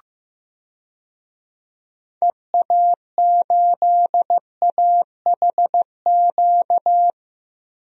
Morse code, heard as a keyed tone, about 15 wpm, 700 Hz. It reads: EA8AHQ
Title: DXCC-100 Call Signs - 6 Characters - Repeated 3x - 15wpm